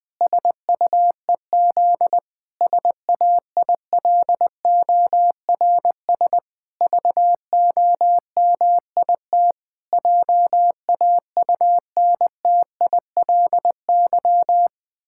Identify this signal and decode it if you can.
Morse code, heard as a keyed tone, about 21 wpm, 700 Hz. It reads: SUEZ SAILORS VOMIT JAUNTILY
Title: Pangram Challenge v2 - 20wpm